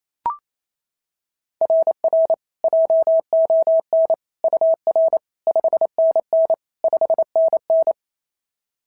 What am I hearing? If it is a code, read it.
Morse code, heard as a keyed tone, about 28 wpm, 690 Hz. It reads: RR JON UR 5NN 5NN
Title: SOTA QSO Practice 28wpm